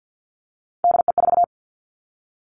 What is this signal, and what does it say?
Morse code, heard as a keyed tone, about 50 wpm, 700 Hz. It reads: DE4
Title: Three Challenging Letter-Number Combinations 50wpm